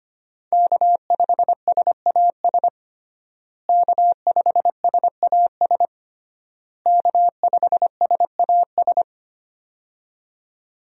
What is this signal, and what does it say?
Morse code, heard as a keyed tone, about 25 wpm, 700 Hz. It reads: K5SAS K5SAS K5SAS
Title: USA Call Signs - 5 Characters - Repeated 3x - 25wpm